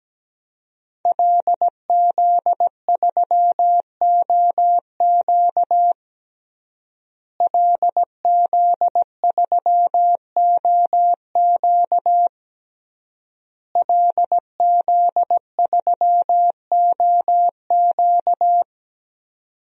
Morse code, heard as a keyed tone, about 17 wpm, 700 Hz. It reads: LZ3OQ LZ3OQ LZ3OQ
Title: DXCC-100 Call Signs - 5 Characters - Repeated 3x - 17wpm